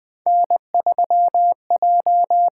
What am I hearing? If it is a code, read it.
Morse code, heard as a keyed tone, about 20 wpm, 700 Hz. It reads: N3J
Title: USA Call Signs - 8x Character Spacing - 20wpm